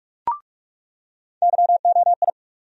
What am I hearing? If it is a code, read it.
Morse code, heard as a keyed tone, about 43 wpm, 700 Hz. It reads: YOI
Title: USA Call Sign Postfixes - 3 Letters 45wpm